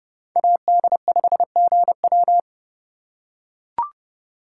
Morse code, heard as a keyed tone, about 31 wpm, 700 Hz. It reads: AD5GW